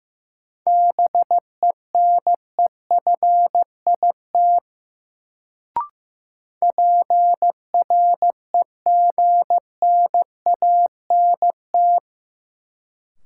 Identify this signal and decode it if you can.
Morse code, heard as a keyed tone, about 15 wpm, 700 Hz. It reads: BENEFIT PREGNANT
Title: Sentences from Top 1000 Words - Review Part 6 - New Word Drill - 15wpm